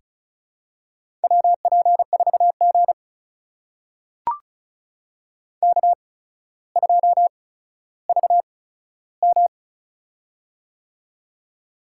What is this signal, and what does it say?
Morse code, heard as a keyed tone, about 36 wpm, spaced as for 15 wpm, 700 Hz. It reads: WP4G K2VM
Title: USA Call Signs - 4 Characters - 8x Character Spacing - 35wpm